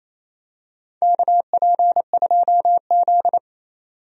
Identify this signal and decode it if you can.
Morse code, heard as a keyed tone, about 28 wpm, 700 Hz. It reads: KP2Z
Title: DXCC-100 Call Signs - 4 Characters - 28wpm